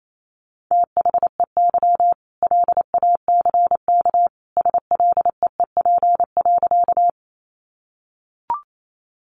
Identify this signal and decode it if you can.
Morse code, heard as a keyed tone, about 28 wpm, 700 Hz. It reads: THEY LACK SLEEP.
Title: Sentences from Top 1100 Words 28wpm